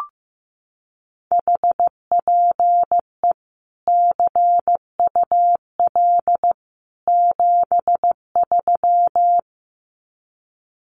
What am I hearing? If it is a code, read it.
Morse code, heard as a keyed tone, about 15 wpm, 700 Hz. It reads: HPE CUL 73
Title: Common QSO Phrases 15wpm